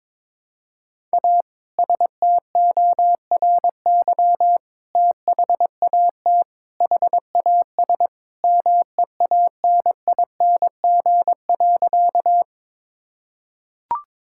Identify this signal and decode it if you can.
Morse code, heard as a keyed tone, about 21 wpm, 700 Hz. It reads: A STORY THAT HAS MEANING.
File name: practice / 22wpm / Sentences with 2-Word Phrases - 22wpm